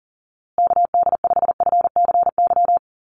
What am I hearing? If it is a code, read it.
Morse code, heard as a keyed tone, about 40 wpm, 700 Hz. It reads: KD5FCY
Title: USA Call Signs Repeated 2x 40wpm